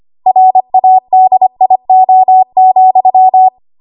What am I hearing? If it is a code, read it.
Morse code, heard as a keyed tone, about 25 wpm, 750 Hz. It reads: RADIO,
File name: Archive / 25 WPM / 25WPM